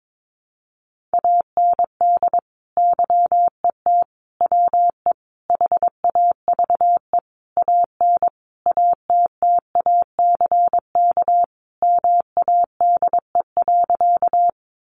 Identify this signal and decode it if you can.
Morse code, heard as a keyed tone, about 22 wpm, 700 Hz. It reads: AND YET WE HAVE AN ATTACK MADE.
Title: Sentences from Top 1500 Words 22wpm